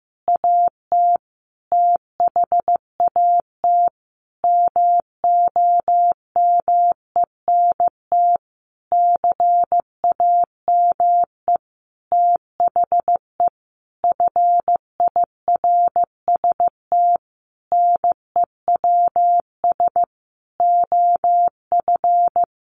Morse code, heard as a keyed tone, about 15 wpm, 700 Hz. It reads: AT THAT MOMENT CAME THE FIRST NEWS OF
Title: Sentences from Top 1500 Words 15wpm